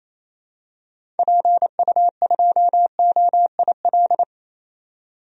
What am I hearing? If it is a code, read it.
Morse code, heard as a keyed tone, about 27 wpm, 700 Hz. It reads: PU2OIL
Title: DXCC-100 Call Signs - 6 Characters - 28wpm